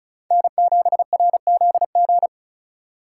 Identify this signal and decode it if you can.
Morse code, heard as a keyed tone, about 44 wpm, 700 Hz. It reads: N7RZG
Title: Speed Racing - USA 1x3 Call Signs 35wpm